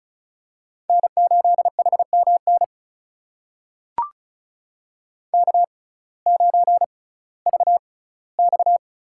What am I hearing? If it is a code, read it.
Morse code, heard as a keyed tone, about 34 wpm, spaced as for 19 wpm, 700 Hz. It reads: N8HMN K9VX
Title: USA Call Signs - 5 Characters - 6x Character Spacing - 35wpm